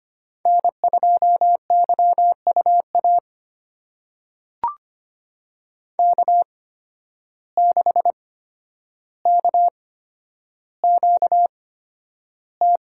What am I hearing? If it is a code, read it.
Morse code, heard as a keyed tone, about 25 wpm, spaced as for 11 wpm, 700 Hz. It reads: N2YUA K6KQT
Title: USA Call Signs - 5 Characters - 8x Character Spacing - 25wpm